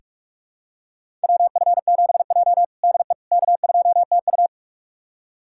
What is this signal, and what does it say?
Morse code, heard as a keyed tone, about 46 wpm, 700 Hz. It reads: WF7J DE K2TU
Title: USA Callsign DE Callsign - 4 Characters - 45wpm